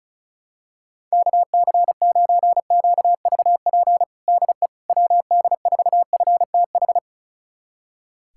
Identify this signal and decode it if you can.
Morse code, heard as a keyed tone, about 36 wpm, 700 Hz. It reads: KC9QVP DE WD4FTH
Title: USA Callsign DE Callsign - 6 Characters - 35wpm